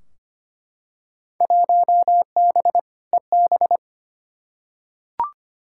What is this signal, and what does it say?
Morse code, heard as a keyed tone, about 25 wpm, 700 Hz. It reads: 1B EB
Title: Field Day Exchanges - 25wpm